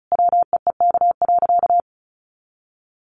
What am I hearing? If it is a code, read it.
Morse code, heard as a keyed tone, about 34 wpm, 700 Hz. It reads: WEEK.
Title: Easy Phrases 35wpm